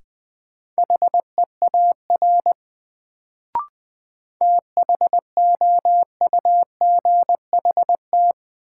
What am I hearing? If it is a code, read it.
Morse code, heard as a keyed tone, about 20 wpm, 700 Hz. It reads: HEAR THOUGHT
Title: Sentences from Top 300 Words - New Word Drill - 20wpm